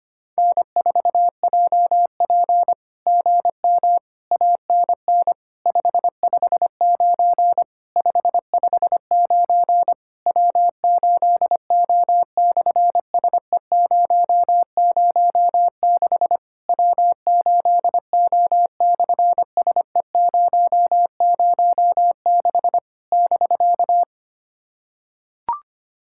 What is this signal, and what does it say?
Morse code, heard as a keyed tone, about 24 wpm, 700 Hz. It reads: N4JP GM ANN 559 559 W8O/SE006 W8O/SE006 <BK>
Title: SOTA QSO Practice 25wpm